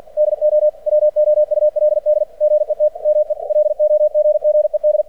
Morse code, heard as a keyed tone, about 50 wpm, 600 Hz. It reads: DO WOULD GET ME LOGGED
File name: archived / CRSnet